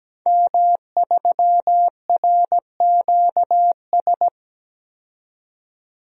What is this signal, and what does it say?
Morse code, heard as a keyed tone, about 17 wpm, 700 Hz. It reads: M3RQS
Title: DXCC-100 Call Signs - 5 Characters - 17wpm